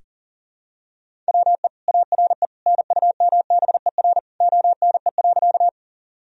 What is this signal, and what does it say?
Morse code, heard as a keyed tone, about 37 wpm, 700 Hz. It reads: WE ARE NUMBER ONE.
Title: Sentences from Top 1000 Words - Review Part 1 - 40wpm